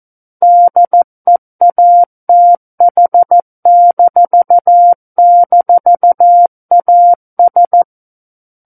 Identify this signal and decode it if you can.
Morse code, heard as a keyed tone, about 14 wpm, 700 Hz. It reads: DEATH--AS
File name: Book One / 05-Chapter 5 - 14 WPM